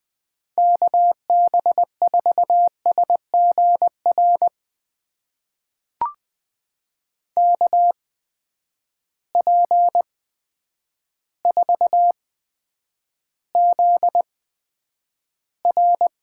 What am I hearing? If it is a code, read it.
Morse code, heard as a keyed tone, about 20 wpm, spaced as for 9 wpm, 700 Hz. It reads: KB4SGR KP4ZR
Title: USA Call Signs - 8x Character Spacing - 20wpm